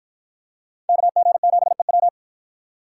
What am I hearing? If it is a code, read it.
Morse code, heard as a keyed tone, about 46 wpm, 700 Hz. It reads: KG7EW